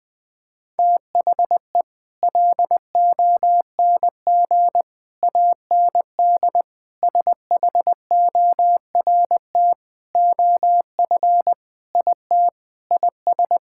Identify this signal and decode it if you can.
Morse code, heard as a keyed tone, about 20 wpm, 700 Hz. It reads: THE LONG AND SHORT OF IT IS
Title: Sets of 8 Words - 20wpm